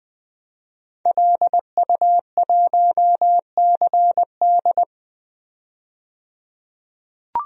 Morse code, heard as a keyed tone, about 20 wpm, 700 Hz. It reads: LU1CD
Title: DXCC-100 Call Signs - 5 Characters - 20wpm